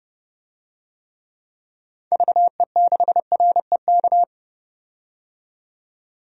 Morse code, heard as a keyed tone, about 30 wpm, 700 Hz. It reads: VE6REK
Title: DXCC-100 Call Signs - 6 Characters - 30wpm